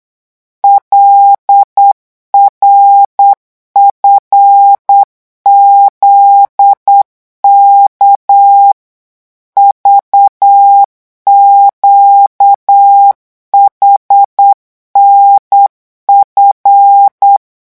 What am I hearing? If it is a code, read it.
Morse code, heard as a keyed tone, about 8 wpm, 800 Hz. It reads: LRFZK VQHNF
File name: morse mp3 / revision8-vitesse 8 mots